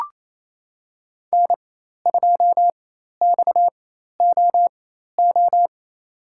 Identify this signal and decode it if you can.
Morse code, heard as a keyed tone, about 27 wpm, spaced as for 13 wpm, 700 Hz. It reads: N2XOO
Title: USA Call Signs - 5 Characters - 4x Character Spacing - 28wpm